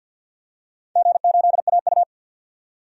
Morse code, heard as a keyed tone, about 47 wpm, 700 Hz. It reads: G8AU
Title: DXCC-100 Call Signs - 4 Characters - 50wpm